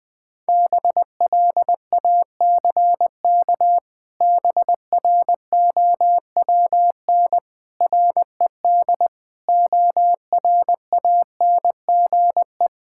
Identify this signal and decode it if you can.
Morse code, heard as a keyed tone, about 20 wpm, 700 Hz. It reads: BLACK BROWN RED ORANGE
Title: Pangram Challenge v2 - 20wpm